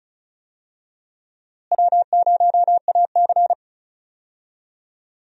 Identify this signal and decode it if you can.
Morse code, heard as a keyed tone, about 36 wpm, 700 Hz. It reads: W0AC
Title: USA Call Signs - 4 Characters - Repeated 2x - 35wpm